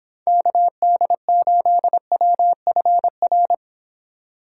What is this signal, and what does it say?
Morse code, heard as a keyed tone, about 26 wpm, 700 Hz. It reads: KD8WFR